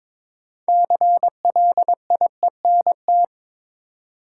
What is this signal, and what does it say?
Morse code, heard as a keyed tone, about 22 wpm, 700 Hz. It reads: CLIENT
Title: Six Letter Words - 22wpm